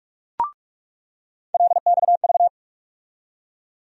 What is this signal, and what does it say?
Morse code, heard as a keyed tone, about 41 wpm, 700 Hz. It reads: LXV
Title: USA Call Sign Postfixes - 3 Letters 45wpm